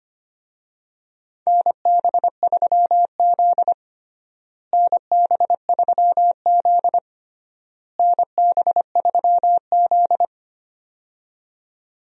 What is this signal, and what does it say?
Morse code, heard as a keyed tone, about 25 wpm, 700 Hz. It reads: NB3Z NB3Z NB3Z
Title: USA Call Signs - 4 Characters - Repeated 3x - 25wpm